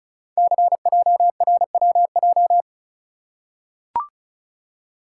Morse code, heard as a keyed tone, about 36 wpm, 690 Hz. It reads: CJRWJ
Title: Code Groups - 5 Letters - 35wpm